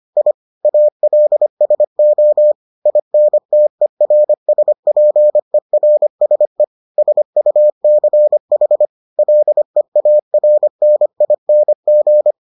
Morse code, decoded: I ALSO INTERSPERSE SUCH LEARNING

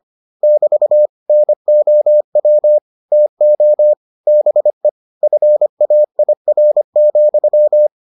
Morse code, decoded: <BT> NOW TO BE FAIR,

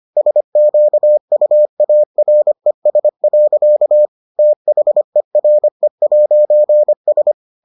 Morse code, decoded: SQUARES. THERE'S

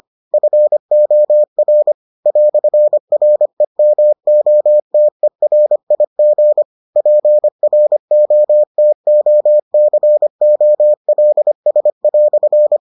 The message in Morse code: FOR "REMOTERIG PROTOCOLS"

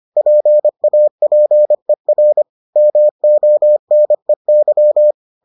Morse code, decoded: PAPER MONEY

